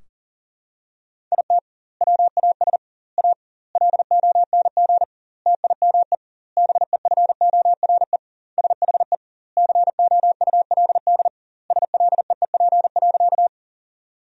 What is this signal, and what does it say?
Morse code, heard as a keyed tone, about 40 wpm, 700 Hz. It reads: IT WAS A LONG TIME BEFORE SHE COULD SLEEP.
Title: 2x Word-Spacing - Sentences from Top 500 Words - 40wpm